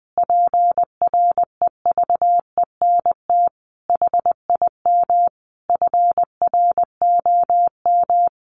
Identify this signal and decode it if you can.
Morse code, heard as a keyed tone, about 20 wpm, 700 Hz. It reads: PREVENT HIM FROM
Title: Sentences from Top 1500 Words 20wpm